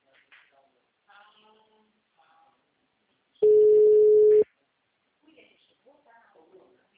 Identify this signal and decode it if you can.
Ringback tone (Europe, 425 Hz), with speech.